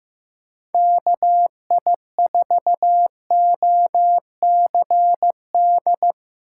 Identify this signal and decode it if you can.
Morse code, heard as a keyed tone, about 15 wpm, 700 Hz. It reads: KI4OCD
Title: USA Call Signs - 6x Character Spacing - 15wpm